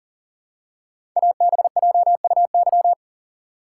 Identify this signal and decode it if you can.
Morse code, heard as a keyed tone, about 41 wpm, 690 Hz. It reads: ABJUY
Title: Code Groups - 5 Letters - 40wpm